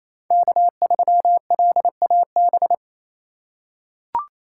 Morse code, heard as a keyed tone, about 27 wpm, 700 Hz. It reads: K3LAB
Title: USA Call Signs - 6x Character Spacing - 28wpm